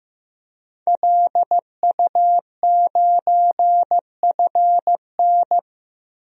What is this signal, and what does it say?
Morse code, heard as a keyed tone, about 15 wpm, 700 Hz. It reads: LU9FN
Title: DXCC-100 Call Signs - 5 Characters - Repeated 3x - 15wpm